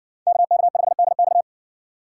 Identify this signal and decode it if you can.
Morse code, heard as a keyed tone, about 60 wpm, 700 Hz. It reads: KK5DX